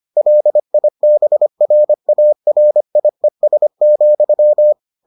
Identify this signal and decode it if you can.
Morse code, heard as a keyed tone, about 25 wpm, 600 Hz. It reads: LIBRARIES,